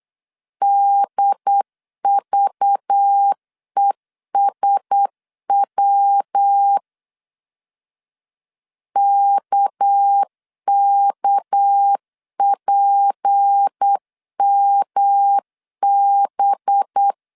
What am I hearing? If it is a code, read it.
Morse code, heard as a keyed tone, about 8 wpm, 800 Hz. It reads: DVESW KKPMB
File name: CW / DET / 1 6 1500C S B 8wpm 5142